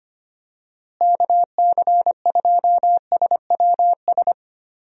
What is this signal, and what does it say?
Morse code, heard as a keyed tone, about 25 wpm, 700 Hz. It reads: KC2SWS
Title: USA Call Signs - 6 Characters - Repeated 3x - 25wpm